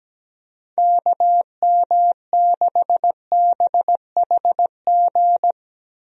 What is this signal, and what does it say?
Morse code, heard as a keyed tone, about 17 wpm, 700 Hz. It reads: KM6BHG
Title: USA Call Signs - 6 Characters - Repeated 3x - 17wpm